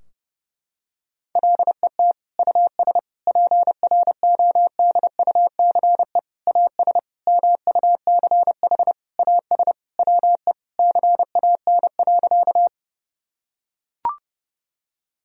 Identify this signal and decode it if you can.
Morse code, heard as a keyed tone, about 30 wpm, 700 Hz. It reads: LET US PRODUCE AS MUCH AS WE CAN.
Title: Sentences from Top 1000 Words - Review Part 3 - 30wpm